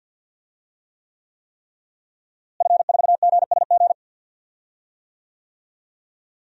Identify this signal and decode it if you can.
Morse code, heard as a keyed tone, about 51 wpm, 700 Hz. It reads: F4GSG
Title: DXCC-100 Call Signs - 5 Characters - Repeated 3x - 50wpm